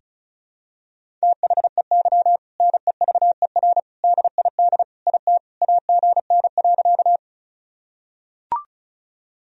Morse code, heard as a keyed tone, about 36 wpm, 700 Hz. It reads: THEY NEVER DID IT AGN.
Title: Sentences from Top 300 Words and Abbreviations 35wpm